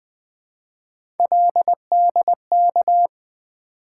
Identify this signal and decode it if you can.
Morse code, heard as a keyed tone, about 20 wpm, 700 Hz. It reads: LDK
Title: Three Letter-Number Combinations - 20wpm